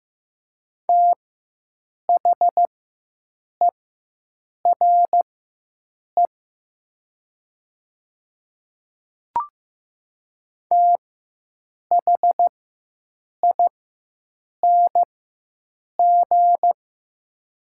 Morse code, heard as a keyed tone, about 15 wpm, spaced as for 7 wpm, 700 Hz. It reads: THERE THING
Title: 4x Character Spacing - Top 100 Words - 15wpm